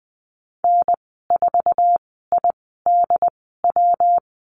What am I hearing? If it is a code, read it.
Morse code, heard as a keyed tone, about 20 wpm, spaced as for 15 wpm, 700 Hz. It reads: N4IDW
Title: 167 Lesson 27g - Number 7 - Call Signs 20wpm